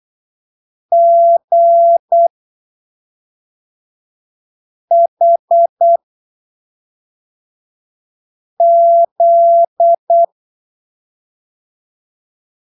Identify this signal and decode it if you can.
Morse code, heard as a keyed tone, about 8 wpm, spaced as for 3 wpm, 680 Hz. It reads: GHZ
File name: arquivos / texto / texto 12 08ppm